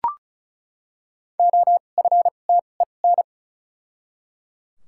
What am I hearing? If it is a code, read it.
Morse code, heard as a keyed tone, about 36 wpm, spaced as for 25 wpm, 700 Hz. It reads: OFTEN
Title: 2x Character Spacing - Top 500 Words - 35wpm